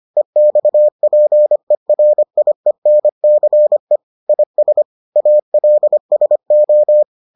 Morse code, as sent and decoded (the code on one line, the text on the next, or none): . -..- .--. .-. .. -. -.-. / .. ... / .- .-.. ... ---
EXPERIENCE IS ALSO